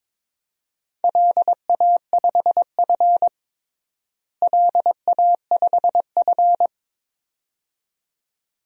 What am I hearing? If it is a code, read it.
Morse code, heard as a keyed tone, about 22 wpm, 700 Hz. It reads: LA5F LA5F
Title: DXCC-100 Call Signs - 4 Characters - Repeated 2x - 22wpm